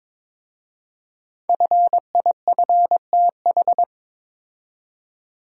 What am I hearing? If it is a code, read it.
Morse code, heard as a keyed tone, about 22 wpm, 700 Hz. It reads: FIFTH
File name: 5 Letter Words - First 3 Letters Given - 22wpm